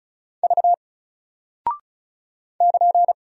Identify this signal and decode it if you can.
Morse code, heard as a keyed tone, about 36 wpm, 700 Hz. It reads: V <KN>
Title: Single Character (Letter:Number:Punctuation:Prosign) 35wpm